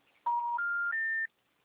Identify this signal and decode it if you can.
Special information tone.